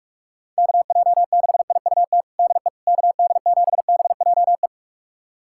Morse code, heard as a keyed tone, about 46 wpm, 700 Hz. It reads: KJ6IUT DE KD7BJE
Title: USA Callsign DE Callsign 45wpm